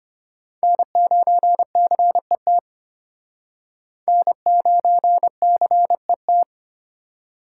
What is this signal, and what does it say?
Morse code, heard as a keyed tone, about 34 wpm, 700 Hz. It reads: N9CET N9CET